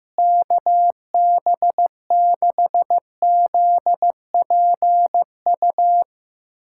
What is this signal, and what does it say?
Morse code, heard as a keyed tone, about 15 wpm, 700 Hz. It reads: KB6ZPU
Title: USA Callsign DE Callsign 15wpm